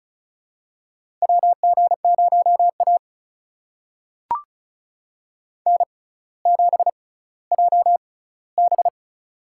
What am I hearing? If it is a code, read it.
Morse code, heard as a keyed tone, about 36 wpm, spaced as for 18 wpm, 700 Hz. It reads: WG0A N7JB
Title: USA Call Signs - 4 Characters - 6x Character Spacing - 35wpm